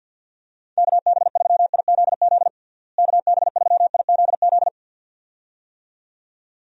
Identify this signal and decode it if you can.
Morse code, heard as a keyed tone, about 51 wpm, 700 Hz. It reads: KB3IZZ KB3IZZ
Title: USA Call Signs - 6 Characters - Repeated 2x - 50wpm